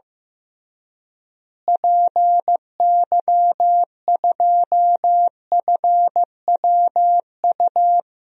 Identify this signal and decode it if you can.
Morse code, heard as a keyed tone, about 15 wpm, 700 Hz. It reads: PY2FWU
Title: DXCC-100 Call Signs - 6 Characters - Repeated 2x - 15wpm